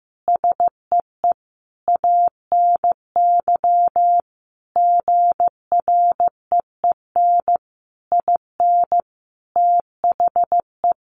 Morse code, heard as a keyed tone, about 15 wpm, 700 Hz. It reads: SEE ANY GREEN IN THE
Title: Sentences from Top 500 Words 15wpm